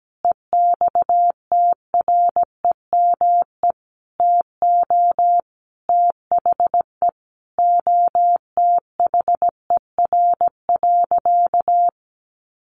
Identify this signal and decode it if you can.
Morse code, heard as a keyed tone, about 17 wpm, 700 Hz. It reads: EXTREME TO THE OTHER.
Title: Sentences from Top 1300 Words 17wpm